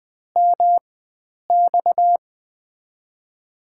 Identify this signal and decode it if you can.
Morse code, heard as a keyed tone, about 20 wpm, spaced as for 9 wpm, 700 Hz. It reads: MX
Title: USA Call Signs - 4x Character Spacing - 20wpm